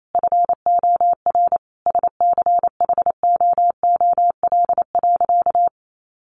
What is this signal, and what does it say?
Morse code, heard as a keyed tone, about 28 wpm, 700 Hz. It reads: FOR SCHOOL.
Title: Sentences from Top 1500 Words 28wpm